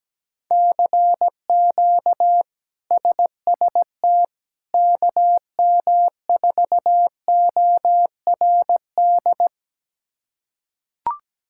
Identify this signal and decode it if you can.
Morse code, heard as a keyed tone, about 17 wpm, 700 Hz. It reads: CQ SST KM4ORD
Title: K1USN SST QSO Practice - 17wpm